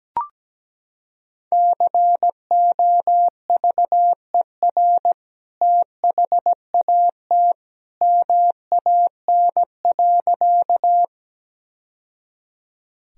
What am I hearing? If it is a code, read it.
Morse code, heard as a keyed tone, about 17 wpm, 700 Hz. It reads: COVER THAT MAN.
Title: Sentences from Top 1000 Words - Review Part 1 - 17wpm